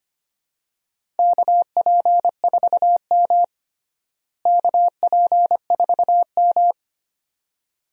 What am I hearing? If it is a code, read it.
Morse code, heard as a keyed tone, about 25 wpm, 700 Hz. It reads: KP4M KP4M
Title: DXCC-100 Call Signs - 4 Characters - Repeated 2x - 25wpm